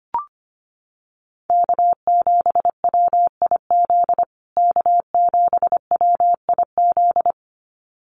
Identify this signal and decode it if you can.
Morse code, heard as a keyed tone, about 25 wpm, 700 Hz. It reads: K7WIZ K7WIZ
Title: USA Call Signs Repeated 2x 25wpm